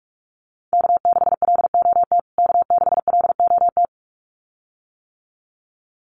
Morse code, heard as a keyed tone, about 44 wpm, 700 Hz. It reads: K6LOT K6LOT
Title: USA Call Signs Repeated 2x 45wpm